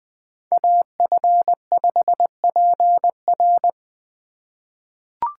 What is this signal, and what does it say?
Morse code, heard as a keyed tone, about 20 wpm, 700 Hz. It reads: AF5PR
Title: USA Call Signs - 5 Characters - 4x Character Spacing - 20wpm